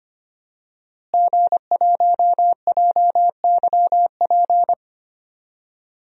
Morse code, heard as a keyed tone, about 25 wpm, 700 Hz. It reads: G1JYP
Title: DXCC-100 Call Signs - 5 Characters - Repeated 3x - 25wpm